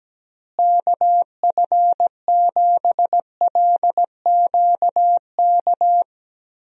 Morse code, heard as a keyed tone, about 17 wpm, 700 Hz. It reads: KF7LQK
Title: USA Call Signs - 6 Characters - Repeated 3x - 17wpm